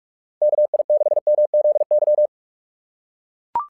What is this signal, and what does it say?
Morse code, heard as a keyed tone, about 44 wpm, 660 Hz. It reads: KI6MZY
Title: SOTA QSO Practice 45wpm